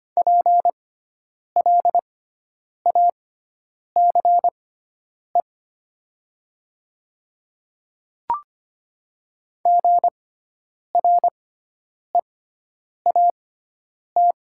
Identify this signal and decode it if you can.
Morse code, heard as a keyed tone, about 25 wpm, spaced as for 9 wpm, 700 Hz. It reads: PLACE GREAT
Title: 6x Character Spacing - Top 500 Words - 25wpm